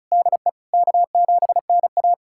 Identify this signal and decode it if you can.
Morse code, heard as a keyed tone, about 36 wpm, 700 Hz. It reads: DE K7NA
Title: USA Callsign DE Callsign - 4 Characters - 35wpm